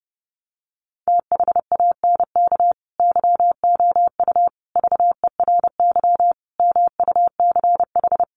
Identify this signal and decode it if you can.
Morse code, heard as a keyed tone, about 30 wpm, 700 Hz. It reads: THANK YOU VERY MUCH
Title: Sets of 5 Words 30wpm